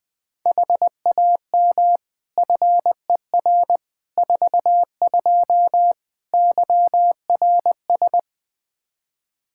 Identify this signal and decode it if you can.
Morse code, heard as a keyed tone, about 20 wpm, 700 Hz. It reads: HAM FER 42 YRS
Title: Practice QSO - Exchange 3 - More Info - 20wpm